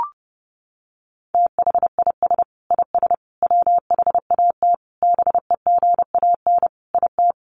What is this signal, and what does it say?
Morse code, heard as a keyed tone, about 30 wpm, 700 Hz. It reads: THIS IS WHAT BEGAN IT
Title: Sentences from Top 300 Words 30wpm